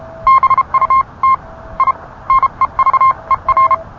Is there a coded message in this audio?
Yes: Morse code from 0.3 to 3.7 s, heard as a keyed tone, about 31 wpm, spaced as for 27 wpm, 1060 Hz. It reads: BUT I NEVER